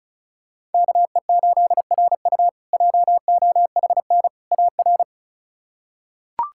Morse code, heard as a keyed tone, about 36 wpm, 700 Hz. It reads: KE8RU JOHN AR
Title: K1USN SST QSO Practice - 35wpm